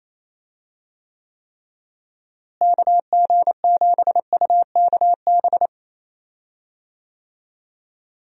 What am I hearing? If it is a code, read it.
Morse code, heard as a keyed tone, about 28 wpm, 700 Hz. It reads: KG7UKB
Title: USA Call Signs - 6 Characters - Repeated 3x - 28wpm